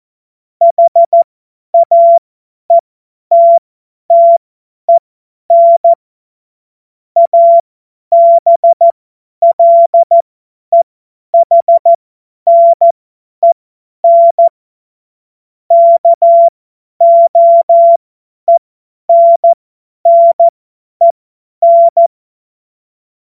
Morse code, heard as a keyed tone, about 14 wpm, spaced as for 10 wpm, 680 Hz. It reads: HAETTEN ABLEHNEN KOENNEN
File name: Morsekurs / Witze / Witz 27. Januar 2026 50 F70